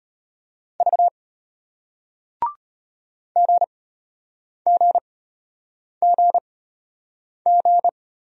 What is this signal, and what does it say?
Morse code, heard as a keyed tone, about 33 wpm, spaced as for 8 wpm, 700 Hz. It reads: V GGGG